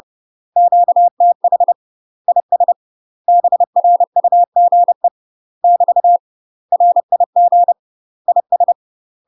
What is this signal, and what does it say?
Morse code, heard as a keyed tone, about 30 wpm, 700 Hz. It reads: QTH IS BRUGE <BT> RIG IS
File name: UK MASTER QSOs TRAINING / 30WPM / UK MASTER QSOs 0028